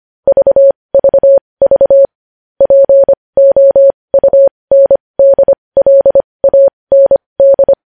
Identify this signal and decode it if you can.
Morse code, heard as a keyed tone, about 25 wpm, spaced as for 20 wpm, 550 Hz. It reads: VVV POUNDLAND
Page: News headlines in Morse code at 20 WPM